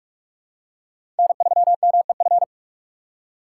Morse code, heard as a keyed tone, about 47 wpm, 700 Hz. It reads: N3MEF
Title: USA Call Signs - 8x Character Spacing - 45wpm